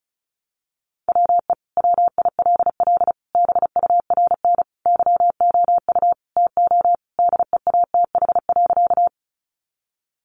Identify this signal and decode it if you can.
Morse code, heard as a keyed tone, about 34 wpm, 700 Hz. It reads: WE WILL BURN YOU TO DEATH.
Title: Sentences from Top 1300 Words 35wpm